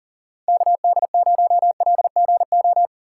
Morse code, heard as a keyed tone, about 40 wpm, 700 Hz. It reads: KD0LGO
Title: USA Call Signs - 6 Characters - 4x Character Spacing - 40wpm